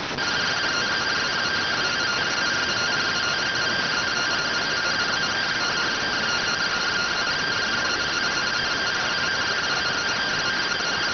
Two-tone teleprinter signal, about 170 Hz shift.